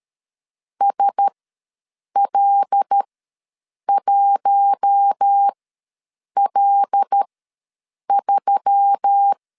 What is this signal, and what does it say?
Morse code, heard as a keyed tone, about 13 wpm, spaced as for 7 wpm, 800 Hz. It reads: SL1L3
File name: LEZIONI / 5 LBVC / 5 L ws 12wpm 2958